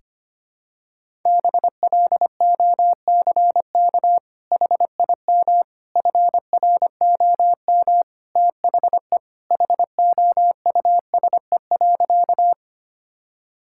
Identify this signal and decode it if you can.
Morse code, heard as a keyed tone, about 25 wpm, 700 Hz. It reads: BLOCK HIM FROM THE HOUSE.
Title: Sentences from Top 1600 Words 25wpm